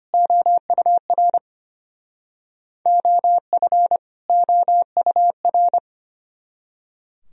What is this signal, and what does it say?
Morse code, heard as a keyed tone, about 32 wpm, 700 Hz. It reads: OUR OF OUR